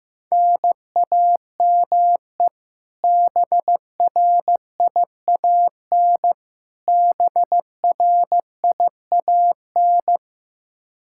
Morse code, heard as a keyed tone, about 16 wpm, 700 Hz. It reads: NAME BRIAN BRIAN
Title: K1USN SST QSO Practice 15wpm